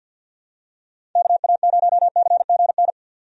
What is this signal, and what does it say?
Morse code, heard as a keyed tone, about 51 wpm, 700 Hz. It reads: KA0CGN
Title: USA Call Signs - 6 Characters - Repeated 3x - 50wpm